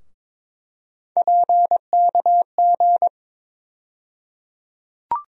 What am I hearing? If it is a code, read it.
Morse code, heard as a keyed tone, about 22 wpm, 700 Hz. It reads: PKG
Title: USA Call Sign Postfixes - 3 Letters 22wpm